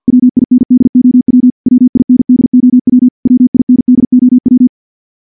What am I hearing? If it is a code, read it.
Morse code, heard as a keyed tone, about 46 wpm, 250 Hz. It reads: WINDOW WINDOW WINDOW
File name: morse / morse-training-50WPM